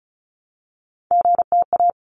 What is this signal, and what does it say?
Morse code, heard as a keyed tone, about 34 wpm, 700 Hz. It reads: GTA
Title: ARRL Sections 35wpm